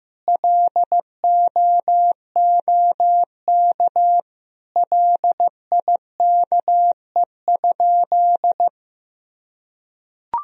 Morse code, heard as a keyed tone, about 15 wpm, 700 Hz. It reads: LOOK LIKE?
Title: Sentences from Top 100 Words and Abbreviations 15wpm